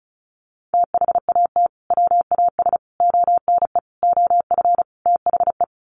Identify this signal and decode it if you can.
Morse code, heard as a keyed tone, about 34 wpm, 700 Hz. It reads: THAT WAS ONE OF THE
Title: Sets of 5 Words 35wpm